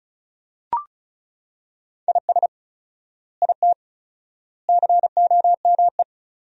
Morse code, heard as a keyed tone, about 34 wpm, 700 Hz. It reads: IS IT COME
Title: 4x Word-Spacing - Sentences from Top 500 Words - 35wpm